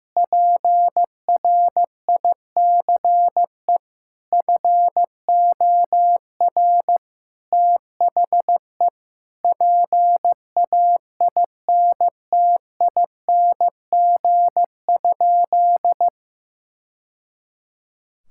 Morse code, heard as a keyed tone, about 15 wpm, 700 Hz. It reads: PRICE FOR THE PAINTING?
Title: Sentences from Top 1000 Words - Review Part 6 - 15wpm